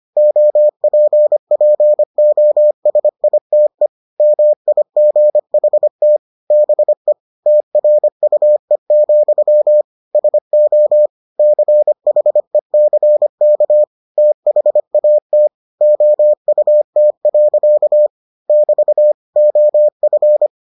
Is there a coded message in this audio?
Yes: Morse code from 0.2 to 20.5 s, heard as a keyed tone, about 25 wpm, 600 Hz. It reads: OPPOSITE MIGHT BE TRUE, SO CHECK THAT OUT. <BT> OF